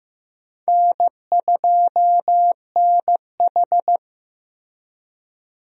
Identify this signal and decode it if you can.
Morse code, heard as a keyed tone, about 15 wpm, 700 Hz. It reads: N2NH
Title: USA Call Signs - 4 Characters - Repeated 2x - 15wpm